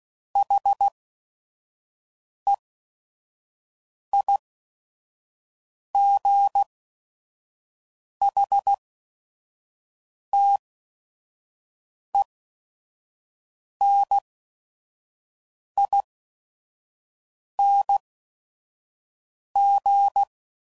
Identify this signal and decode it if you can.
Morse code, heard as a keyed tone, about 16 wpm, spaced as for 5 wpm, 800 Hz. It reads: HEIGHTENING
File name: Suffixes / Special ING's / VerySlowToFaster ING's